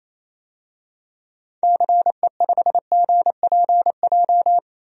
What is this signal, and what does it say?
Morse code, heard as a keyed tone, about 28 wpm, 700 Hz. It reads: CE5GPJ
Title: DXCC-100 Call Signs - 6 Characters - 28wpm